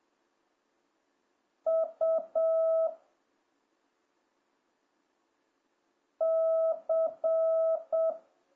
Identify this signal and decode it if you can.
Morse code, heard as a keyed tone, about 7 wpm, 650 Hz. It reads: U C